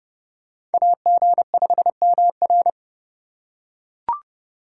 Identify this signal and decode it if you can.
Morse code, heard as a keyed tone, about 30 wpm, 700 Hz. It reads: AG5MR